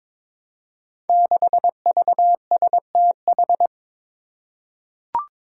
Morse code, heard as a keyed tone, about 22 wpm, 700 Hz. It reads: 6VSTH
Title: Five Challenging Letter-Number Combinations 22wpm